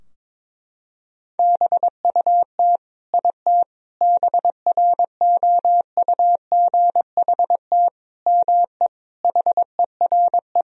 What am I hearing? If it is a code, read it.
Morse code, heard as a keyed tone, about 22 wpm, 700 Hz. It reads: BUT IT BROUGHT ME HERE
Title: Sentences from Top 1000 Words - Review Part 4 - 22wpm